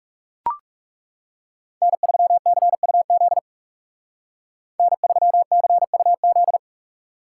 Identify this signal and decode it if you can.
Morse code, heard as a keyed tone, about 36 wpm, 700 Hz. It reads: N3CUZ N3CUZ
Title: Speed Racing - USA Call Signs 30wpm